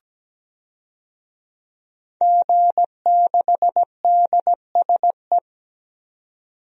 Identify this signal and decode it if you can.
Morse code, heard as a keyed tone, about 17 wpm, 700 Hz. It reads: G6DSE
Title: DXCC-100 Call Signs - 5 Characters - 17wpm